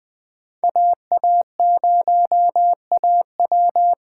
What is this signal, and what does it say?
Morse code, heard as a keyed tone, about 20 wpm, 700 Hz. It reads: AA0AW
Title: Speed Racing - USA 2x2 Call Signs 17wpm